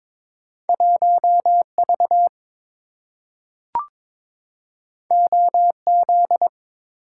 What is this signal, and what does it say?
Morse code, heard as a keyed tone, about 22 wpm, 700 Hz. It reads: 1V OZ
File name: practice / 22wpm / Two Letter-Number Combinations - 22wpm